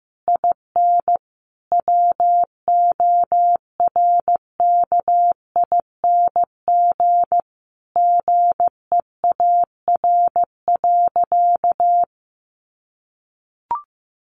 Morse code, heard as a keyed tone, about 15 wpm, 700 Hz. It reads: IN WORKING GEAR.
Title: Sentences from Top 1400 Words 15wpm